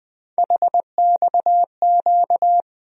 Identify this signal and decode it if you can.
Morse code, heard as a keyed tone, about 20 wpm, 700 Hz. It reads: HXQ
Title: USA Call Sign Postfixes - 3 Letters 20wpm